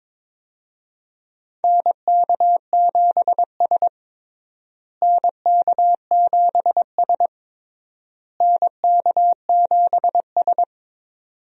Morse code, heard as a keyed tone, about 22 wpm, 700 Hz. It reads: NK7S NK7S NK7S
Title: USA Call Signs - 4 Characters - Repeated 3x - 22wpm